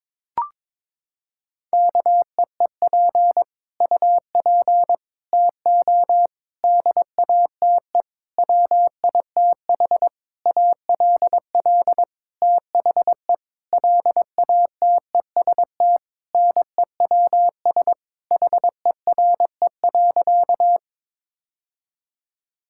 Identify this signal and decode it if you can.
Morse code, heard as a keyed tone, about 22 wpm, 700 Hz. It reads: KEEP UP TO DATE WITH ALL THE LATEST NEWS HERE.
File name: Sentences with 9-Word Phrases - 22wpm